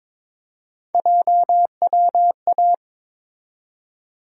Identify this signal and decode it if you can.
Morse code, heard as a keyed tone, about 22 wpm, 700 Hz. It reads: JWA
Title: USA Call Sign Postfixes - 3 Letters 22wpm